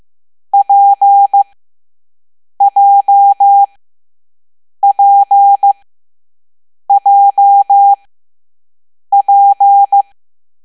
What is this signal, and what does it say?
Morse code, heard as a keyed tone, about 15 wpm, spaced as for 6 wpm, 800 Hz. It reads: PJPJP
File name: Morse / Level7B